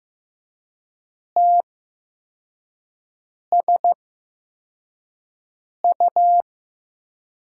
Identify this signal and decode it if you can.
Morse code, heard as a keyed tone, about 15 wpm, spaced as for 4 wpm, 700 Hz. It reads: TSU
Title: USA Call Signs - 6 Characters - 8x Character Spacing - 15wpm